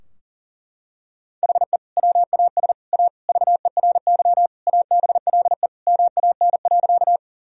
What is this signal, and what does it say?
Morse code, heard as a keyed tone, about 40 wpm, 700 Hz. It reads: HE WAS A VERY ABLE MAN.
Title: Sentences from Top 1000 Words - Review Part 2 - 40wpm